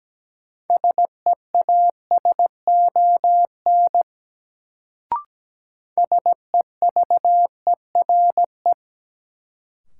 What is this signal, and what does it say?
Morse code, heard as a keyed tone, about 17 wpm, 700 Hz. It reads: SEASON SEVERE
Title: Sentences from Top 1400 Words - New Word Drill Alt - 17wpm